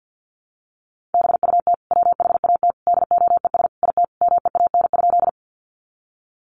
Morse code, heard as a keyed tone, about 50 wpm, 700 Hz. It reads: BUT WHAT DOES IT MEAN?
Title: Sentences from Top 200 Words 50wpm